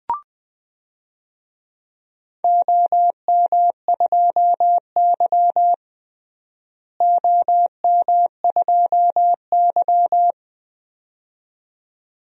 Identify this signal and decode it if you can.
Morse code, heard as a keyed tone, about 20 wpm, 700 Hz. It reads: OM2Y OM2Y
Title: DXCC-100 Call Signs - 4 Characters - Repeated 2x - 20wpm